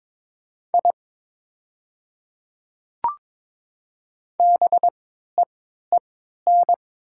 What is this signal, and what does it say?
Morse code, heard as a keyed tone, about 22 wpm, spaced as for 12 wpm, 700 Hz. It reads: I BEEN
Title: 3x Character Spacing - Top 100 Words - 22wpm